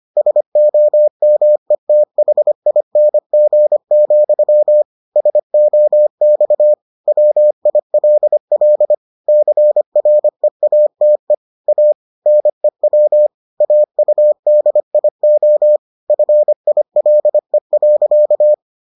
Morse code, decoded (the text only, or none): SOMETHING, SOX WILL CREATE A NEW AUDIO FILE.